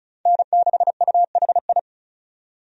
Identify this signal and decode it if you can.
Morse code, heard as a keyed tone, about 36 wpm, 700 Hz. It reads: N6UHI
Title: USA Call Signs - 5 Characters - 35wpm